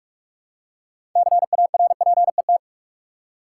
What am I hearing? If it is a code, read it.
Morse code, heard as a keyed tone, about 47 wpm, 700 Hz. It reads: CARPET
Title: Six Letter Words - 45wpm